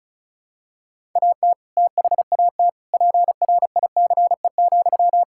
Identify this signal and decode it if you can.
Morse code, heard as a keyed tone, about 34 wpm, 700 Hz. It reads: AT THAT PRICE,
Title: Sentences from Top 1000 Words - Review Part 5 - 35wpm